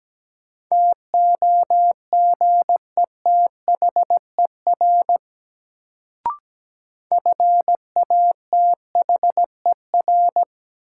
Morse code, heard as a keyed tone, about 17 wpm, 700 Hz. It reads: TOGETHER FATHER
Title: Sentences from Top 300 Words - New Word Drill - 17wpm